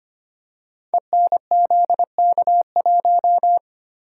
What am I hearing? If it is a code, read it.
Morse code, heard as a keyed tone, about 25 wpm, 700 Hz. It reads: ENZK1
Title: Code Groups - 5 Letter:Numbers - 25wpm